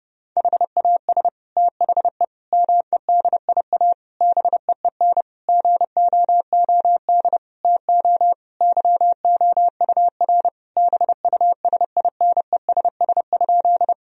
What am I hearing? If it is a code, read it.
Morse code, heard as a keyed tone, about 30 wpm, 700 Hz. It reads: HAS THE MEDIA BEEN GOOD TO YOUR BUSINESS?
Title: Sentences from Top 1000 Words - Review Part 4 - 30wpm